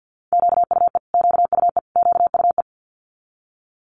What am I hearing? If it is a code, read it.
Morse code, heard as a keyed tone, about 50 wpm, 700 Hz. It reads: QUE QUE QUE
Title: Multi-letter Phonemes Repeated 3x 50wpm